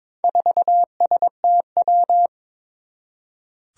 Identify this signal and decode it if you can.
Morse code, heard as a keyed tone, about 22 wpm, 700 Hz. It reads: 4STW
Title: Four Challenging Letter-Number Combinations 22wpm